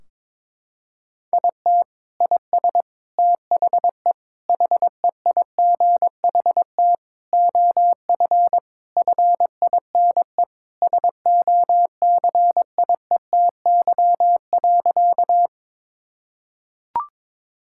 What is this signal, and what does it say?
Morse code, heard as a keyed tone, about 21 wpm, 700 Hz. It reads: IT IS THE HEIGHT OF FINE SOCIETY.
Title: Sentences from Top 1600 Words 22wpm